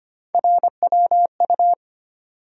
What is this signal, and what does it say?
Morse code, heard as a keyed tone, about 25 wpm, 700 Hz. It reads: RWU
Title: USA Call Sign Postfixes - 3 Letters 25wpm